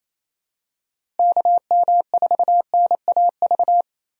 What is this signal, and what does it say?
Morse code, heard as a keyed tone, about 28 wpm, 700 Hz. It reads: KM4NAV
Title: USA Call Signs - 6 Characters - Repeated 3x - 28wpm